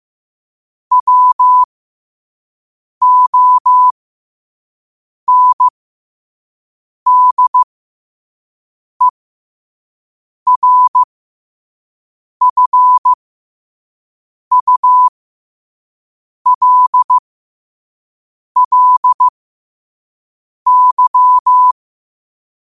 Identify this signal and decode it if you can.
Morse code, heard as a keyed tone, about 15 wpm, spaced as for 5 wpm, 1000 Hz. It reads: WONDERFULLY
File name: qotdfiles / 5wpm